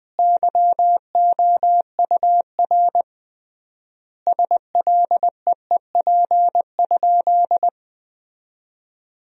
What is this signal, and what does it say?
Morse code, heard as a keyed tone, about 20 wpm, 700 Hz. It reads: YOUR SLEEP?
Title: 3x Word-Spacing - Sentences from Top 500 Words - 20wpm